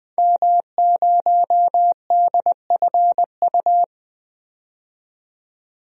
Morse code, heard as a keyed tone, about 20 wpm, 700 Hz. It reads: M0DFU
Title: DXCC-100 Call Signs - 5 Characters - 20wpm